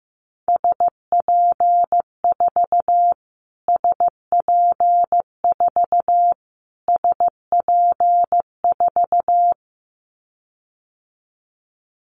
Morse code, heard as a keyed tone, about 14 wpm, 700 Hz. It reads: SP4 SP4 SP4
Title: Callsign Prefixes for Top 25 DXCC - Repeated 3x - 15wpm